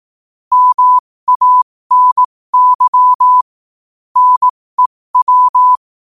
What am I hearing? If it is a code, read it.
Morse code, heard as a keyed tone, about 18 wpm, spaced as for 15 wpm, 1000 Hz. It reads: MANY NEW
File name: qotdfiles / 15wpm